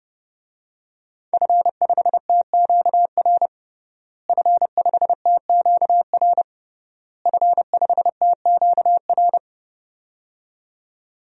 Morse code, heard as a keyed tone, about 30 wpm, 700 Hz. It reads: F5TQR F5TQR F5TQR
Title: DXCC-100 Call Signs - 5 Characters - Repeated 3x - 30wpm